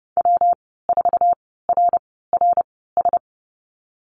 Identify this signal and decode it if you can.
Morse code, heard as a keyed tone, about 30 wpm, spaced as for 17 wpm, 700 Hz. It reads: W4RRS
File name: Lesson 19g - Letter w - Call Signs 30wpm